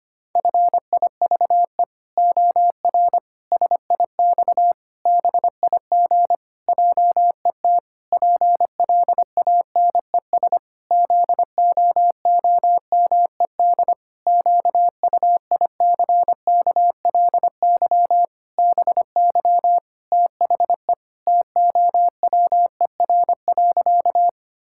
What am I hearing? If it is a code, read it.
Morse code, heard as a keyed tone, about 25 wpm, 700 Hz. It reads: FIVE OR SIX BIG JET PLANES ZOOMED QUICKLY BY THE TOWER.
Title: Pangram Challenge v2 - 25wpm